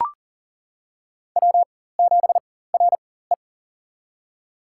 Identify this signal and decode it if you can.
Morse code, heard as a keyed tone, about 38 wpm, spaced as for 18 wpm, 700 Hz. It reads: W7RE
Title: USA Call Signs - 4 Characters - 4x Character Spacing - 40wpm